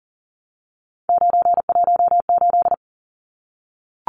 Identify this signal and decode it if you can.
Morse code, heard as a keyed tone, about 40 wpm, 700 Hz. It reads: 918
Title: Numbers - 3-Digit 40wpm